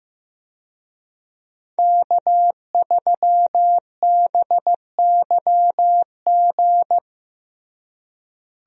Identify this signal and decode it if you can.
Morse code, heard as a keyed tone, about 15 wpm, 700 Hz. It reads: K3BYG
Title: USA Call Signs - 5 Characters - Repeated 3x - 15wpm